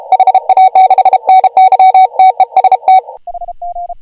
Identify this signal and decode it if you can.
Morse code, heard as a keyed tone, about 33 wpm, 750 Hz. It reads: HA6NY TEST
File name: AUDIO2019 / HG1G CW